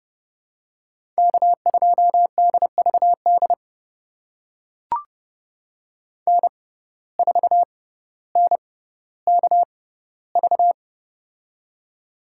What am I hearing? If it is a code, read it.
Morse code, heard as a keyed tone, about 30 wpm, spaced as for 15 wpm, 700 Hz. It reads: K2DVD N4NKV
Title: USA Call Signs - 5 Characters - 6x Character Spacing - 30wpm